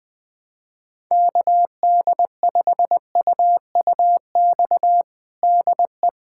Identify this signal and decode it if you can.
Morse code, heard as a keyed tone, about 20 wpm, 700 Hz. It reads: KD5UUX DE
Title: USA Callsign DE Callsign 20wpm